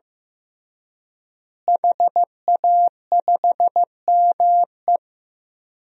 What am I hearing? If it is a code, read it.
Morse code, heard as a keyed tone, about 15 wpm, 700 Hz. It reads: HA5ME
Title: DXCC-100 Call Signs - 5 Characters - Repeated 3x - 15wpm